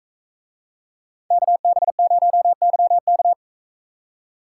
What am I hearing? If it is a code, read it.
Morse code, heard as a keyed tone, about 40 wpm, 700 Hz. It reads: KB0YK